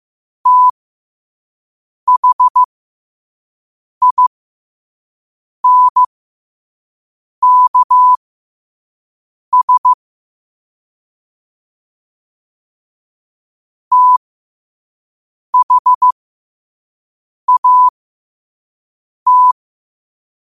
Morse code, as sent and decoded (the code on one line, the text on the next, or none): - .... .. -. -.- ... / - .... .- -
THINKS THAT